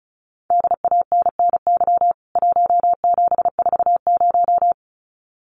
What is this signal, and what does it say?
Morse code, heard as a keyed tone, about 34 wpm, 700 Hz. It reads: DANNY 1740
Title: CWT Exchange 35wpm